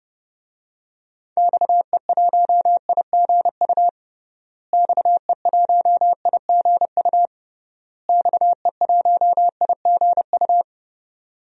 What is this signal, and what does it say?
Morse code, heard as a keyed tone, about 30 wpm, 700 Hz. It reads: XE1IGU XE1IGU XE1IGU
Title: DXCC-100 Call Signs - 6 Characters - Repeated 3x - 30wpm